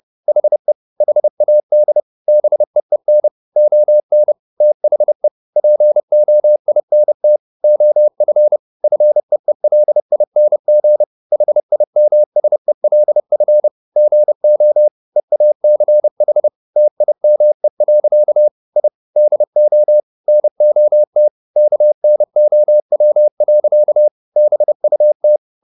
Morse code, heard as a keyed tone, about 30 wpm, 600 Hz. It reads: HE HAD BEEN ON THE POINT OF FEELING HIMSELF GO EACH TIME. I DO NOT KNOW. BUT